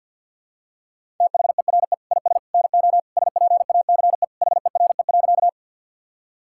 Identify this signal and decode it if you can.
Morse code, heard as a keyed tone, about 52 wpm, 700 Hz. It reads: THERE IS NO SPACE HERE.
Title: Sentences from Top 1000 Words - Review Part 5 - 50wpm